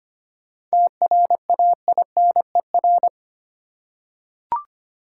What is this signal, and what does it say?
Morse code, heard as a keyed tone, about 25 wpm, 700 Hz. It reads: TRAINER
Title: Seven Letter Words - 25wpm